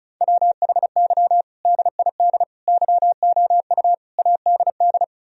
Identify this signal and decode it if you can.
Morse code, heard as a keyed tone, about 34 wpm, 700 Hz. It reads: WHY DID YOU ADD
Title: Sentences from Top 1000 Words - Review Part 1 - 35wpm